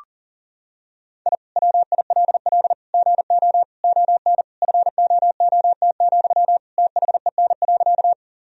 Morse code, heard as a keyed tone, about 40 wpm, 700 Hz. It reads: I WILL GO ON FOOT, THEN.
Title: Sentences from Top 1000 Words - Review Part 3 - 40wpm